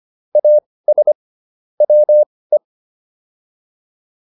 Morse code, heard as a keyed tone, about 25 wpm, spaced as for 18 wpm, 600 Hz. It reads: AS WE